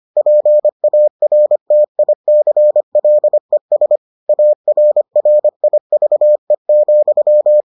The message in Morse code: PARTICLES ARRIVE,

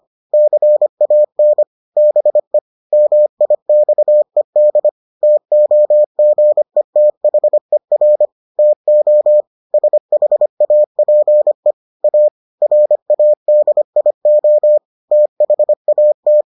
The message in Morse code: CAN BE MIXED TOGETHER TO SHAPE A RADIO THAT

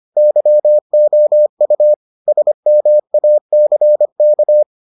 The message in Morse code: YOU SMACK

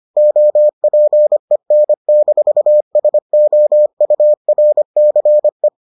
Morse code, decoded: OPEN-SOURCE

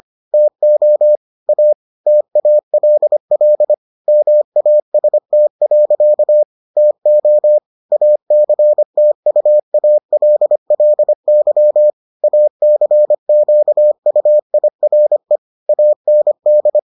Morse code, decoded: TO A TALL MAST. TO ACTUALLY ACQUIRE AND

